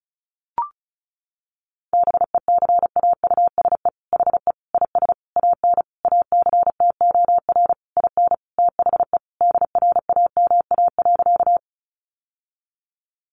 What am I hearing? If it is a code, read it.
Morse code, heard as a keyed tone, about 34 wpm, 700 Hz. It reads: BECAUSE HE IS AN ACTOR IN THE DRAMA.
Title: Sentences from Top 1500 Words 35wpm